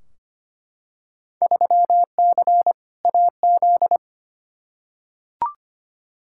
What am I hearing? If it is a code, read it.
Morse code, heard as a keyed tone, about 25 wpm, 700 Hz. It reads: 3C AZ
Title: Field Day Exchanges Encore - 25wpm